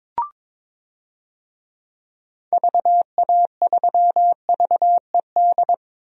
Morse code, heard as a keyed tone, about 22 wpm, 700 Hz. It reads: VA3VED
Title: DXCC-100 Call Signs - 6 Characters - 22wpm